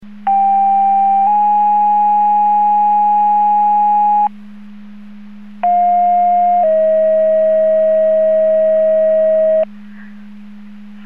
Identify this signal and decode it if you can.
Two-tone page, heard as a one-second tone, then a three-second tone, about 790 then 830 Hz, about 710 then 640 Hz. It is heard from 0.3 to 9.6 s.
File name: Portage / 00099 09-00 Portage 154.385 Paging